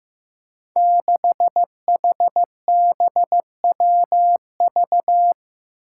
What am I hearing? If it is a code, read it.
Morse code, heard as a keyed tone, about 16 wpm, 700 Hz. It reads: 6HBWV
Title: Five Challenging Letter-Number Combinations 15wpm